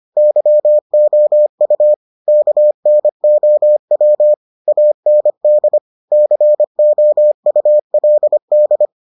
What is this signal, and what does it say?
Morse code, heard as a keyed tone, about 25 wpm, 600 Hz. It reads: YOU KNOW AND COULD